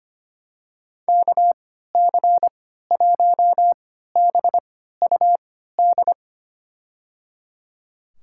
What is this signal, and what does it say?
Morse code, heard as a keyed tone, about 25 wpm, spaced as for 14 wpm, 700 Hz. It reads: KC1BUD
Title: USA Call Signs - 6 Characters - 3x Character Spacing - 25wpm